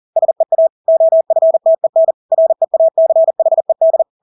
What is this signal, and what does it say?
Morse code, heard as a keyed tone, about 40 wpm, 650 Hz. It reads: SEA OFTEN REACHED
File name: Engelske ord 40-40 wpm